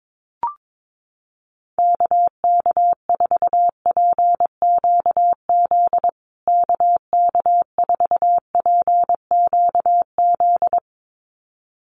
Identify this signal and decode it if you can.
Morse code, heard as a keyed tone, about 22 wpm, 700 Hz. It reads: KK4PQZ KK4PQZ
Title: USA Call Signs Repeated 2x 22wpm